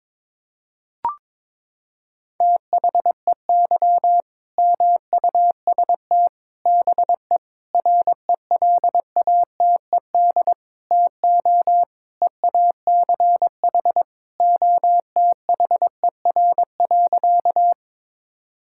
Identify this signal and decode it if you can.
Morse code, heard as a keyed tone, about 21 wpm, 700 Hz. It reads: THEY MUST BE RELATED TO EACH OTHER.
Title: Sentences from Top 1000 Words - Review Part 5 - 22wpm